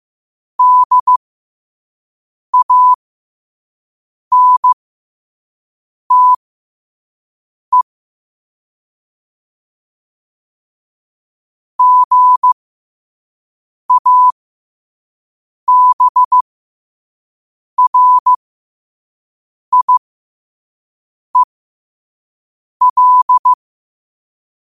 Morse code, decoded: DANTE GABRIEL

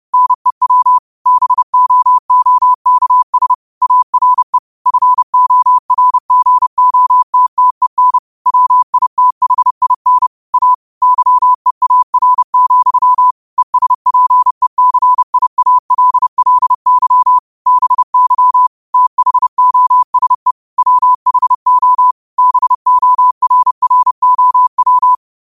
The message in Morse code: NEW BOOKS ARE FORGOTTEN WITHIN A YEAR, ESPECIALLY BY THOSE WHO BORROW

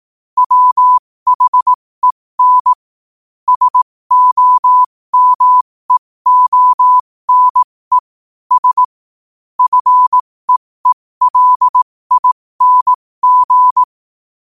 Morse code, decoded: WHEN SOMEONE S FEELING

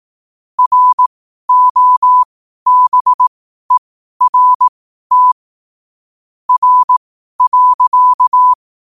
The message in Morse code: ROBERT R.

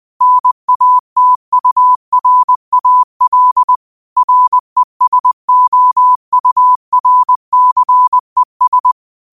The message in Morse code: NATURAL RESOURCES

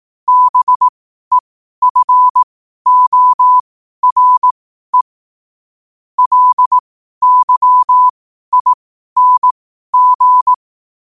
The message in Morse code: BEFORE LYING

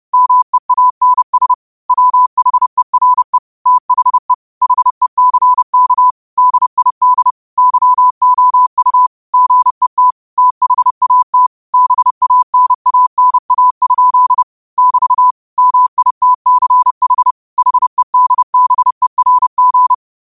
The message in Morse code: MEANS WHERE THE HECK DID YOU GET THAT BANANA? <BT> MITCH HEDBERG

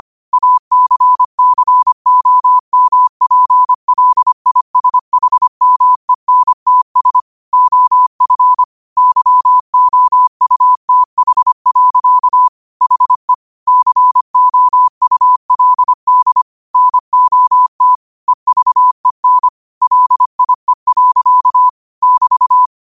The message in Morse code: ACCOMPLISHMENTS OF YOUTH. HE COULD NOT EVEN LIE. <BT>

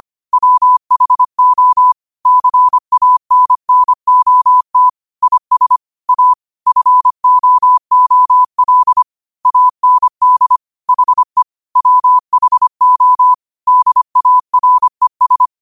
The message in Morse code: WHO CANNOT IS A FOOL AND HE WHO DARES

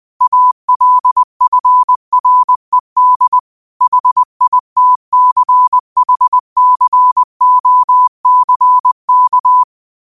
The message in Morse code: ALFRED HITCHCOCK

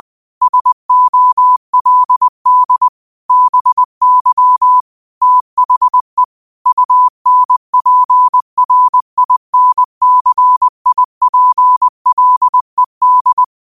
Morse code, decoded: SOLD BY THE UNPRINCIPLED

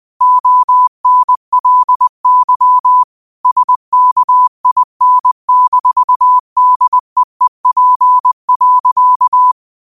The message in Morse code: ONLY SKIN-DEEP.

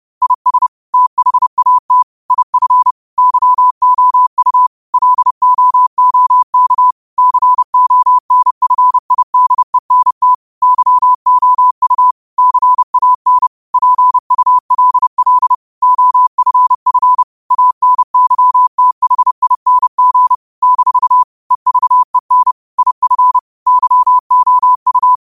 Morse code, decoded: IS THAT IF YOU LOOK CONFIDENT YOU CAN PULL OFF ANYTHING - EVEN IF YOU

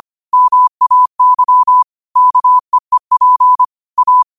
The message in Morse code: MAY KEEP A

